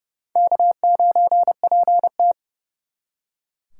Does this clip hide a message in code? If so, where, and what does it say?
Yes: Morse code from 0.4 to 2.3 s, heard as a keyed tone, about 38 wpm, 700 Hz. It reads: K9PT